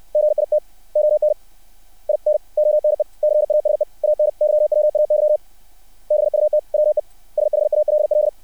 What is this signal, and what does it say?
Morse code, heard as a keyed tone, about 60 wpm, 600 Hz. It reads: GIT ON IN ONE CASE ANYWAY BUT WE SHALL
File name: archived / CRSnet